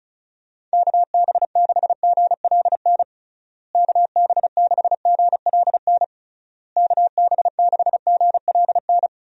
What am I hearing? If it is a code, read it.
Morse code, heard as a keyed tone, about 36 wpm, 700 Hz. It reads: KB6GLN KB6GLN KB6GLN
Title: USA Call Signs - 6 Characters - Repeated 3x - 35wpm